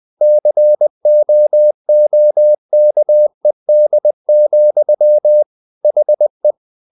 Morse code, heard as a keyed tone, about 20 wpm, 600 Hz. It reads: COOKED, HE
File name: the old man and the sea part 2 20wpm